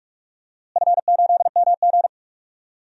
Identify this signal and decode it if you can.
Morse code, heard as a keyed tone, about 42 wpm, 700 Hz. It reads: F8MG
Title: DXCC-100 Call Signs - 4 Characters - Repeated 3x - 45wpm